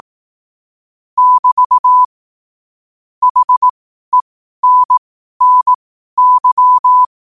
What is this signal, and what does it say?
Morse code, heard as a keyed tone, about 18 wpm, spaced as for 12 wpm, 1000 Hz. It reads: <BT> HENNY
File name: qotdfiles / 12wpm